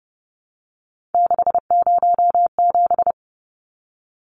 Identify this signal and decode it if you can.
Morse code, heard as a keyed tone, about 30 wpm, 700 Hz. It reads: 607
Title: Numbers - 3-Digit 30wpm